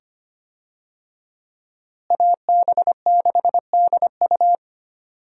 Morse code, heard as a keyed tone, about 25 wpm, 700 Hz. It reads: AB6DU
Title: USA Call Signs - 5 Characters - Repeated 2x - 25wpm